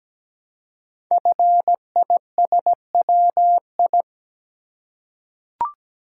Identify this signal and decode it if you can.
Morse code, heard as a keyed tone, about 17 wpm, 700 Hz. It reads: FISWI
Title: Code Groups - 5 Letters - 17wpm